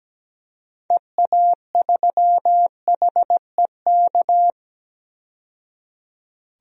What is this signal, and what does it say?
Morse code, heard as a keyed tone, about 17 wpm, 700 Hz. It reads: EA3HEK
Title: DXCC-100 Call Signs - 6 Characters - 17wpm